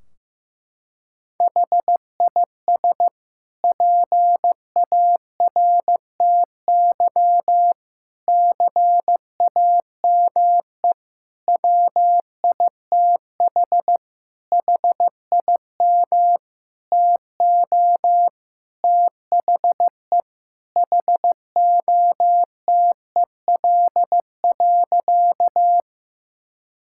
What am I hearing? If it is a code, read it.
Morse code, heard as a keyed tone, about 15 wpm, 700 Hz. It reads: HIS PARTY CAME WITH HIM TO THE HOTEL.
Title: Sentences from Top 1600 Words 15wpm